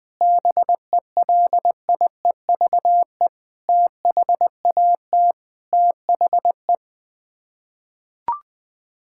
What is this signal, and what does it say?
Morse code, heard as a keyed tone, about 20 wpm, 700 Hz. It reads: BELIEVE THAT THE
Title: Sets of 8 Words - 20wpm